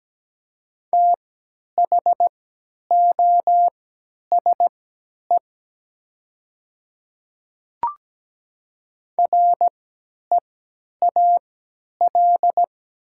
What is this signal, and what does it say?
Morse code, heard as a keyed tone, about 17 wpm, spaced as for 10 wpm, 700 Hz. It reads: THOSE REAL
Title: 3x Character Spacing - Top 300 Words - 17wpm